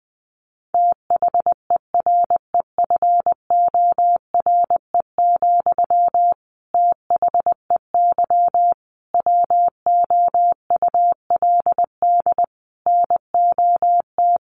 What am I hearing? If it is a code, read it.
Morse code, heard as a keyed tone, about 20 wpm, 700 Hz. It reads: THEREFORE, THEY WOULD NOT
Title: Sentences from Top 800 Words 20wpm